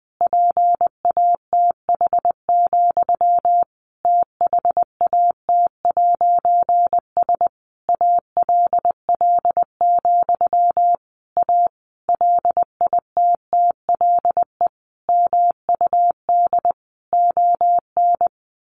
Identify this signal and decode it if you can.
Morse code, heard as a keyed tone, about 19 wpm, 700 Hz. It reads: PATH, THAT'S ALL, A LITTLE MUD ON
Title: A Sound Of Thunder 20wpm